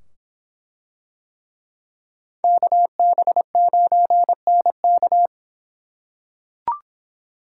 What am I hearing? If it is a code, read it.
Morse code, heard as a keyed tone, about 25 wpm, 700 Hz. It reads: KB9NK